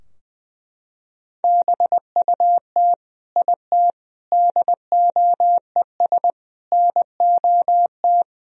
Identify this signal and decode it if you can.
Morse code, heard as a keyed tone, about 20 wpm, 700 Hz. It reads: BUT IT DOES NOT
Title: Sentences from Top 1000 Words - Review Part 2 - 20wpm